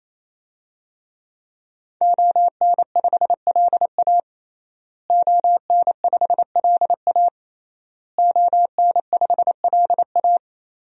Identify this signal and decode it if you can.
Morse code, heard as a keyed tone, about 28 wpm, 700 Hz. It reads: ON5LA ON5LA ON5LA
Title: DXCC-100 Call Signs - 5 Characters - Repeated 3x - 28wpm